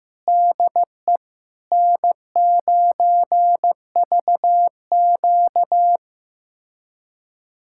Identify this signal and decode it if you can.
Morse code, heard as a keyed tone, about 15 wpm, 700 Hz. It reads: DE N9VQ
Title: USA Callsign DE Callsign - 4 Characters - 15wpm